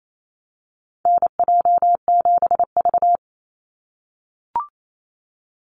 Morse code, heard as a keyed tone, about 28 wpm, 700 Hz. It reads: NJ7V
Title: NA SOTA Activator Callsigns 28wpm